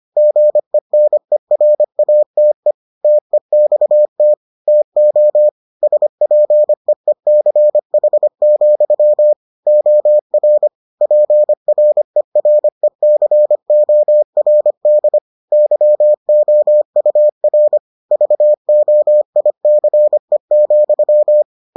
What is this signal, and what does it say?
Morse code, heard as a keyed tone, about 25 wpm, 600 Hz. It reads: GENERATE TEXT TO SPEECH, OR PRERECORD YOUR VOICE,